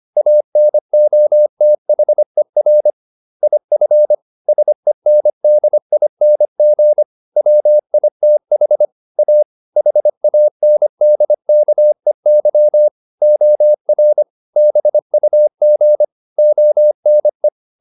Morse code, heard as a keyed tone, about 25 wpm, 600 Hz. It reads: ANOTHER IF SENDING WITH A HANDKEY OR BUG ONE